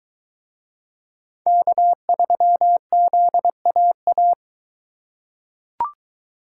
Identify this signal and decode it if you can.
Morse code, heard as a keyed tone, about 22 wpm, 700 Hz. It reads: K3ZAA